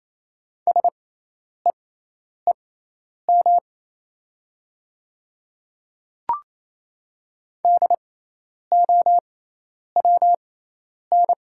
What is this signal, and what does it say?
Morse code, heard as a keyed tone, about 27 wpm, spaced as for 10 wpm, 700 Hz. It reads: SEEM DOWN
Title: 6x Character Spacing - Top 300 Words - 28wpm